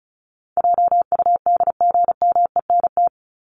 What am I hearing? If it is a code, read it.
Morse code, heard as a keyed tone, about 34 wpm, 700 Hz. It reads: JUDGMENT
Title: Top 1751-2000 Words 35wpm